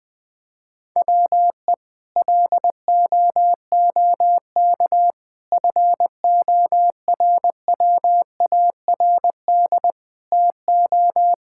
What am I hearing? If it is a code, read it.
Morse code, heard as a keyed tone, about 21 wpm, 700 Hz. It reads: WE LOOK FORWARD TO
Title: Sentences with 6-Word Phrases - 20wpm